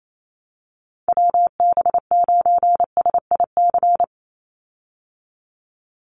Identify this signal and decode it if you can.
Morse code, heard as a keyed tone, about 28 wpm, 700 Hz. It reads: WB9SIC
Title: USA Call Signs 28wpm